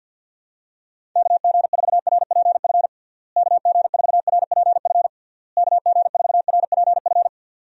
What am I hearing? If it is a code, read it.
Morse code, heard as a keyed tone, about 50 wpm, 700 Hz. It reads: KG4RPF KG4RPF KG4RPF
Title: USA Call Signs - 6 Characters - Repeated 3x - 50wpm